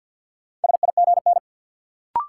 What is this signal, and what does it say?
Morse code, heard as a keyed tone, about 48 wpm, 700 Hz. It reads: SIGN
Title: Sentences from Top 1000 Words - New Word Drill - 50wpm